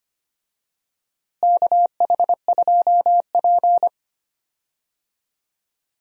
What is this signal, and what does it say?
Morse code, heard as a keyed tone, about 25 wpm, 700 Hz. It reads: KH2P
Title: USA Call Signs - 4 Characters - Repeated 2x - 25wpm